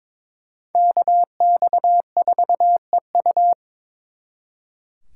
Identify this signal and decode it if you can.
Morse code, heard as a keyed tone, about 26 wpm, 700 Hz. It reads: KX4EU
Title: Speed Racing - USA 2x2 Call Signs 22wpm